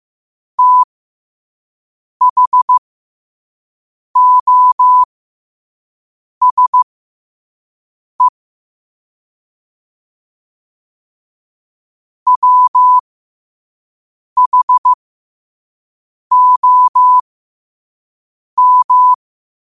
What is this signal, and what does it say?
Morse code, heard as a keyed tone, about 15 wpm, spaced as for 5 wpm, 1000 Hz. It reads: THOSE WHOM
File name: qotdfiles / 5wpm